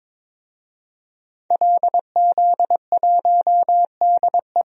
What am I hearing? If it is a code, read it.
Morse code, heard as a keyed tone, about 22 wpm, 700 Hz. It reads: LZ1DE
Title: DXCC-100 Call Signs - 5 Characters - 22wpm